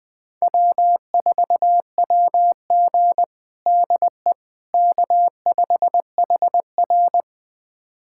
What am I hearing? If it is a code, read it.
Morse code, heard as a keyed tone, about 20 wpm, 700 Hz. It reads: W4WG DE K5HR
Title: USA Callsign DE Callsign - 4 Characters - 20wpm